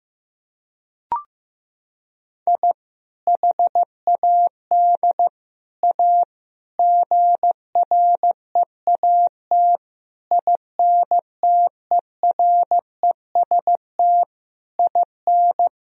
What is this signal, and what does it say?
Morse code, heard as a keyed tone, about 16 wpm, 700 Hz. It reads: I HAD A GREAT INTEREST IN
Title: Sentences from Top 1000 Words - Review Part 2 - 15wpm